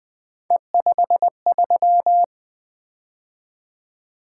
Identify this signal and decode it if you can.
Morse code, heard as a keyed tone, about 20 wpm, 700 Hz. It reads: E53
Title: Three Letter-Number Combinations - 20wpm